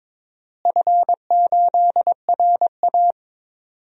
Morse code, heard as a keyed tone, about 22 wpm, 700 Hz. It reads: F8RA
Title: DXCC-100 Call Signs - 4 Characters - Repeated 2x - 22wpm